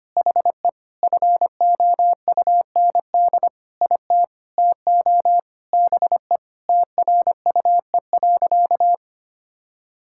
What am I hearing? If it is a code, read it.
Morse code, heard as a keyed tone, about 25 wpm, 700 Hz. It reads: HE FOUND IT TO BE TRUE.
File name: practice / 25wpm / Sentences with 4-Word Phrases - 25wpm